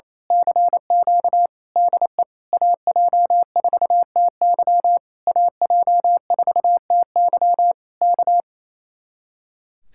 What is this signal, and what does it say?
Morse code, heard as a keyed tone, about 26 wpm, 700 Hz. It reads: CQ DE AJ4TY AJ4TY K
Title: Speed-Racing Common QSO Phrases Encore - 28wpm